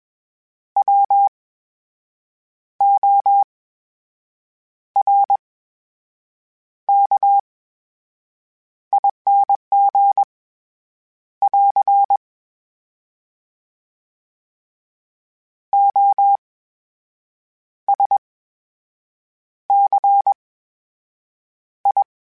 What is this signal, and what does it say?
Morse code, heard as a keyed tone, about 21 wpm, spaced as for 6 wpm, 800 Hz. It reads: WORKING <AR> OSCI
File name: Special INGs-mp3s / Special INGs-mp3s / VerySlowToFaster Ing @20